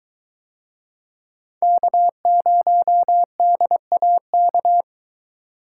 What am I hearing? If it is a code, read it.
Morse code, heard as a keyed tone, about 22 wpm, 700 Hz. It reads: K0DAK